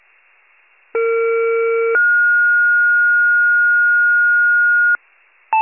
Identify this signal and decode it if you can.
Two-tone page, heard as a one-second tone, then a three-second tone, about 460 then 1430 Hz. It is heard from 0.9 to 5.0 s.
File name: pager / 21 FIRE / wcalert